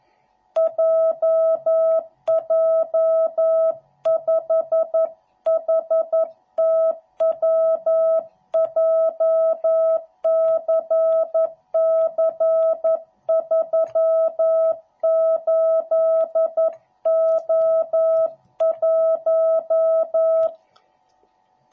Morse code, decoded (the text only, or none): JJ5HTW JCC38O1